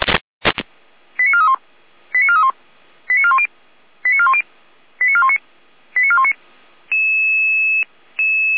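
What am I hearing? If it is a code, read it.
Five-tone selective call (ZVEI), heard as quick runs of five tones, tones of about 70 ms. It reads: 89431 89431 89410 89410 89419 89419
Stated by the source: ZVEI five-tone call 89001 received by SDR, from feuerwehr-orlamuende.org (Orlamünde fire brigade) audio archive.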